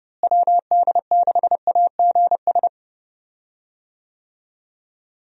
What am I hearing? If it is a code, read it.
Morse code, heard as a keyed tone, about 32 wpm, 700 Hz. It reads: WD6AGS
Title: USA Call Signs - 6 Characters - Repeated 3x - 30wpm